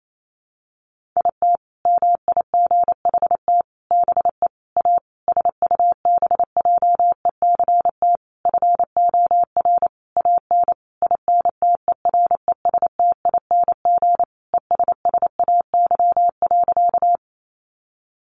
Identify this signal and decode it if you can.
Morse code, heard as a keyed tone, about 28 wpm, 700 Hz. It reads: IT MIGHT BE A SUBJECT FOR AN INTERESTING ESSAY.
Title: Sentences from Top 1500 Words 28wpm